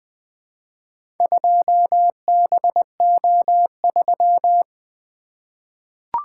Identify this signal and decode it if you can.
Morse code, heard as a keyed tone, about 20 wpm, 700 Hz. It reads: 2BO3
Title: Four Letter-Number Combinations - 20wpm